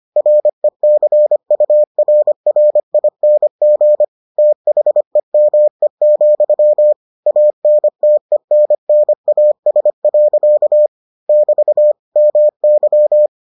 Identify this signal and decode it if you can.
Morse code, heard as a keyed tone, about 25 wpm, 600 Hz. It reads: RECURRING THEME, ANTENNAS. <BT> MY